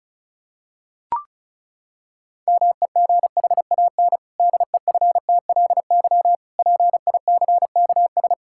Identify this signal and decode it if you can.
Morse code, heard as a keyed tone, about 36 wpm, 700 Hz. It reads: MEGHAN DEFTLY PICKS
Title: Pangram Challenge v2 - 35wpm